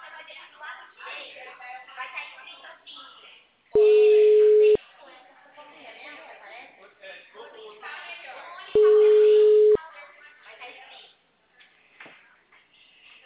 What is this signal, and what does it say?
Ringback tone (Europe, 425 Hz), with speech.